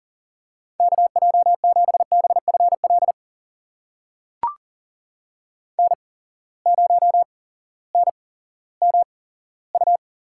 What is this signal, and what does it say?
Morse code, heard as a keyed tone, about 40 wpm, spaced as for 18 wpm, 700 Hz. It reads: KJ7BFL N0NMU
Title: USA Call Signs - 8x Character Spacing - 40wpm